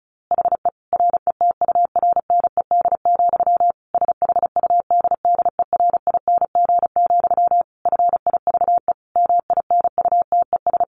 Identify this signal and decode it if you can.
Morse code, heard as a keyed tone, about 34 wpm, 700 Hz. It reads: HE RETURNED, SHUDDERING, FIVE MINUTES
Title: A Sound Of Thunder 35wpm